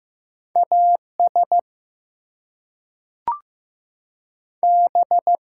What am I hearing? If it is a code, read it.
Morse code, heard as a keyed tone, about 15 wpm, 700 Hz. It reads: AS B
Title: Two Letter Words - 15wpm